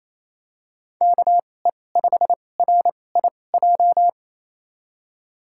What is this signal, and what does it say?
Morse code, heard as a keyed tone, about 28 wpm, spaced as for 20 wpm, 700 Hz. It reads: KE5RIJ
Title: USA Call Signs - 6 Characters - 2x Character Spacing - 28wpm